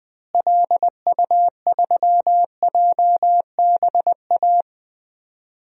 Morse code, heard as a keyed tone, about 20 wpm, 700 Hz. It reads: LU3JBA
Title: DXCC-100 Call Signs - 6 Characters - Repeated 2x - 20wpm